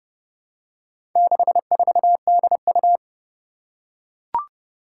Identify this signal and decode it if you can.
Morse code, heard as a keyed tone, about 29 wpm, 700 Hz. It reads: 64DU
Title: Four Challenging Letter-Number Combinations 30wpm